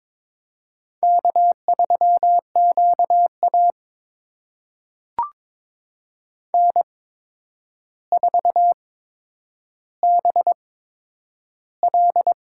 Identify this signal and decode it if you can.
Morse code, heard as a keyed tone, about 22 wpm, spaced as for 9 wpm, 700 Hz. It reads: K3QA N4BL
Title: USA Call Signs - 4 Characters - 8x Character Spacing - 22wpm